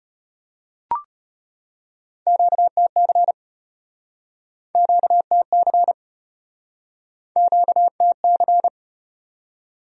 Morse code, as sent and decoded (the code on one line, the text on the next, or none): --.- - -.-. / --.- - -.-. / --.- - -.-.
QTC QTC QTC